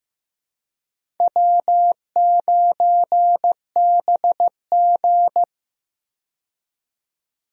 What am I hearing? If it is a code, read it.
Morse code, heard as a keyed tone, about 15 wpm, 700 Hz. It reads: W9BG
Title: USA Call Signs - 4 Characters - 15wpm